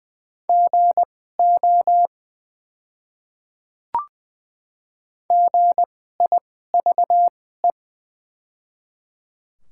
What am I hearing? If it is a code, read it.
Morse code, heard as a keyed tone, about 20 wpm, spaced as for 14 wpm, 700 Hz. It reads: GO GIVE
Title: 2x Character Spacing - Top 300 Words - 20wpm